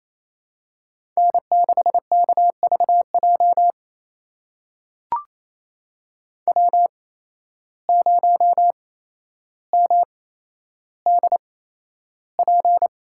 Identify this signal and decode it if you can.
Morse code, heard as a keyed tone, about 28 wpm, spaced as for 12 wpm, 700 Hz. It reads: N6KVJ W0MDP
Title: USA Call Signs - 5 Characters - 8x Character Spacing - 28wpm